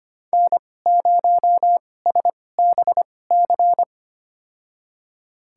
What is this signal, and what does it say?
Morse code, heard as a keyed tone, about 25 wpm, spaced as for 18 wpm, 700 Hz. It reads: N0SBC
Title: USA Call Signs - 5 Characters - 2x Character Spacing - 25wpm